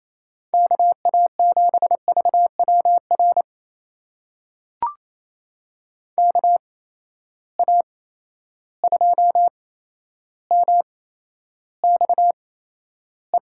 Morse code, heard as a keyed tone, about 28 wpm, spaced as for 12 wpm, 700 Hz. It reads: KA7VWR KA2MXE
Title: USA Call Signs - 6 Characters - 8x Character Spacing - 28wpm